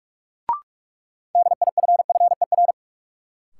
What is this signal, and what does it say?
Morse code, heard as a keyed tone, about 46 wpm, 700 Hz. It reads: DIFFER
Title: Sentences from Top 1100 Words - New Word Drill Alt - 45wpm